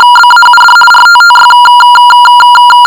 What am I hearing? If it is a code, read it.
RTTY, about 50 baud. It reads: BAUD RTTY
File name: shortwave audio / RTTY 50 baud sample